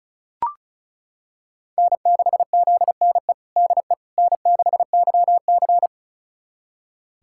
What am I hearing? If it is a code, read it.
Morse code, heard as a keyed tone, about 36 wpm, 700 Hz. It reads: N6ZNE DE N6YC
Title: USA Callsign DE Callsign 35wpm